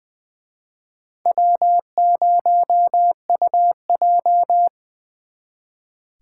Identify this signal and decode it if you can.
Morse code, heard as a keyed tone, about 23 wpm, 700 Hz. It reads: W0UJ
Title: Speed Racing - USA Call Signs 20wpm